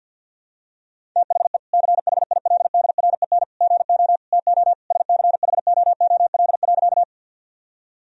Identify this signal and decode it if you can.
Morse code, heard as a keyed tone, about 48 wpm, 700 Hz. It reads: THE CHILDREN GO TO SCHOOL.
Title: Sentences from Top 1000 Words - Review Part 1 - 50wpm